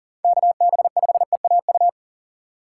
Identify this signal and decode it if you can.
Morse code, heard as a keyed tone, about 40 wpm, 700 Hz. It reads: KB5EAU
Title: USA Call Signs - 6 Characters - 2x Character Spacing - 40wpm